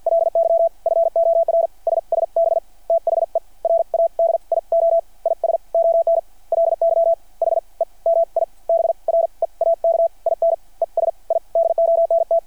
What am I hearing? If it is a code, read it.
Morse code, heard as a keyed tone, about 53 wpm, spaced as for 41 wpm, 690 Hz. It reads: LY FOR SSB THE RADIO IS ONLY HE MI BREAK IN ES I DONT
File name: archived / CRSnet